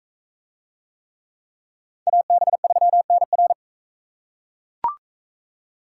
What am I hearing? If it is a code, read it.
Morse code, heard as a keyed tone, about 40 wpm, 700 Hz. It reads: AB3NR